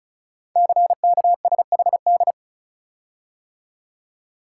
Morse code, heard as a keyed tone, about 36 wpm, 700 Hz. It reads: CKSHD
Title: Code Groups - 5 Letters - 35wpm